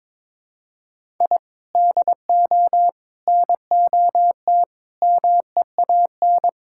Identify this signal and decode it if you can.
Morse code, heard as a keyed tone, about 22 wpm, 700 Hz. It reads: I DO NOT MEAN
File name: practice / 22wpm / Build A Phrase - 5-Word Phrases - 22wpm